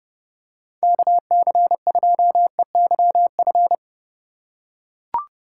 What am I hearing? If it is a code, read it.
Morse code, heard as a keyed tone, about 31 wpm, 700 Hz. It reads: KC2EYF
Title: USA Call Signs - 6 Characters - 4x Character Spacing - 30wpm